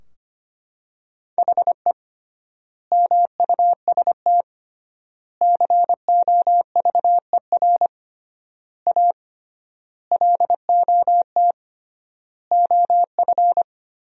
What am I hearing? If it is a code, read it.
Morse code, heard as a keyed tone, about 25 wpm, 700 Hz. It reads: HE MUST COVER A LOT OF
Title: 3x Word-Spacing - Sentences from Top 500 Words - 25wpm